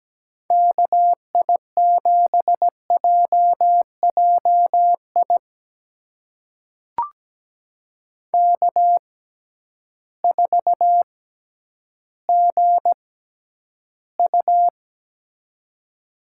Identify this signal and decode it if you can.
Morse code, heard as a keyed tone, about 17 wpm, spaced as for 10 wpm, 700 Hz. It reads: KI7JJI K4GU
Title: USA Call Signs - 6x Character Spacing - 17wpm